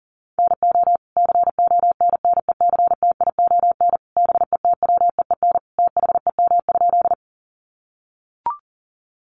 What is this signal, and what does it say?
Morse code, heard as a keyed tone, about 40 wpm, 700 Hz. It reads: NO CONNECTION BETWEEN THEM?
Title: Sentences from Top 1300 Words 40wpm